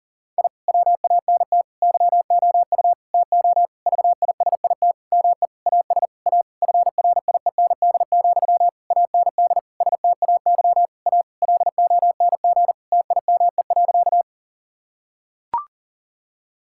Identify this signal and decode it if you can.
Morse code, heard as a keyed tone, about 40 wpm, 690 Hz. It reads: I WANT YOU TO VISIT ME AS A FRIEND, AND STAY A LONG TIME.
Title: Sentences from Top 1000 Words - Review Part 6 - 40wpm